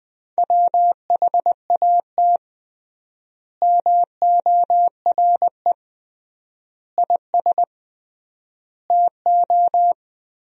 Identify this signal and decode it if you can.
Morse code, heard as a keyed tone, about 21 wpm, 700 Hz. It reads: WHAT MORE IS TO
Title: 3x Word-Spacing - Sentences from Top 500 Words - 20wpm